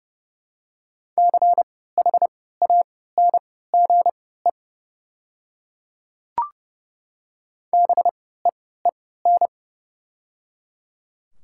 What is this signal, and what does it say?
Morse code, heard as a keyed tone, about 30 wpm, spaced as for 17 wpm, 700 Hz. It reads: CHANGE BEEN
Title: 3x Character Spacing - Top 300 Words - 30wpm